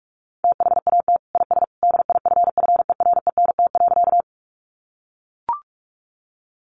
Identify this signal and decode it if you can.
Morse code, heard as a keyed tone, about 44 wpm, 700 Hz. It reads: THAT IS DIFFERENT.
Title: Sentences from Top 600 Words 45wpm